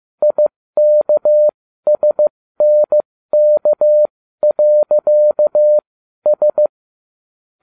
Morse code, heard as a keyed tone, about 15 wpm, spaced as for 13 wpm, 600 Hz. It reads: IKSNK.S